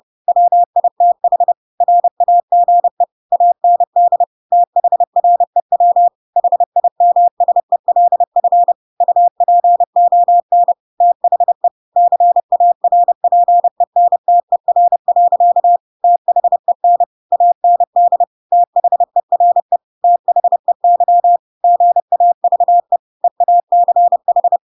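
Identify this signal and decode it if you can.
Morse code, heard as a keyed tone, about 30 wpm, 700 Hz. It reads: WITH RAGE AND THREW HIMSELF UPON THE CARPENTER. THEN AND THERE THEY GAVE EACH
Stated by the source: Chapter 2 of The Adventures of Pinocchio read as a book in Morse code (CW) at 30 wpm, by ik1hge.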